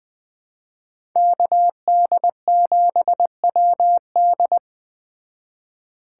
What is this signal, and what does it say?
Morse code, heard as a keyed tone, about 20 wpm, 700 Hz. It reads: KD7WD
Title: Speed Racing - USA 2x2 Call Signs 17wpm